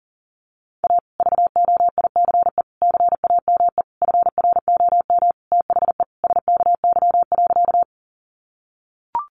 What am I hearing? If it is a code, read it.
Morse code, heard as a keyed tone, about 40 wpm, 700 Hz. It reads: A VOICE CAME FROM THE SKY.
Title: Sentences from Top 1200 Words 40wpm